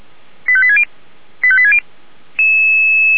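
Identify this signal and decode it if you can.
Five-tone selective call (ZVEI), heard as quick runs of five tones, tones of about 70 ms. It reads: 85860 85860
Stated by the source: ZVEI five-tone selective call 85730 received on SDR.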